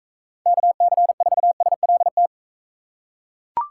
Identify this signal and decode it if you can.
Morse code, heard as a keyed tone, about 42 wpm, 700 Hz. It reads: KC4SLT